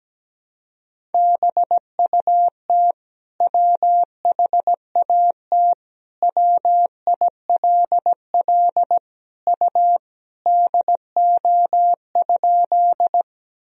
Morse code, heard as a keyed tone, about 17 wpm, 700 Hz. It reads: BUT WHAT WILL U DO?
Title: Sentences from Top 100 Words and Abbreviations 17wpm